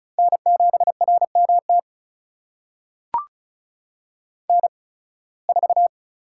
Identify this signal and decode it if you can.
Morse code, heard as a keyed tone, about 36 wpm, spaced as for 23 wpm, 700 Hz. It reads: N7RMT N4
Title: USA Call Signs - 5 Characters - 8x Character Spacing - 35wpm